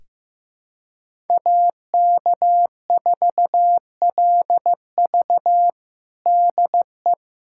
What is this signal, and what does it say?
Morse code, heard as a keyed tone, about 15 wpm, 700 Hz. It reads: AK4LV DE
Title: USA Callsign DE Callsign - 5 Characters - 15wpm